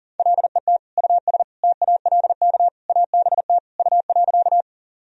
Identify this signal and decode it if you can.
Morse code, heard as a keyed tone, about 40 wpm, 700 Hz. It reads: LET US TALK ABT U.
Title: Sentences from Top 400 Words and Abbreviations 40wpm